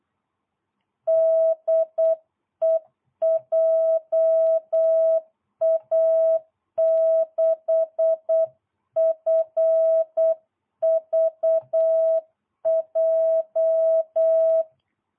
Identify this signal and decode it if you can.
Morse code, heard as a keyed tone, about 8 wpm, 660 Hz. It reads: DE JA6FVJ